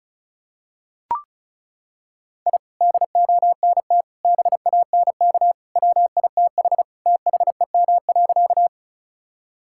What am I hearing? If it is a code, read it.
Morse code, heard as a keyed tone, about 34 wpm, 700 Hz. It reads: I DONT BANK WITH THEM.
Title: Sentences from Top 1000 Words - Review Part 6 - 35wpm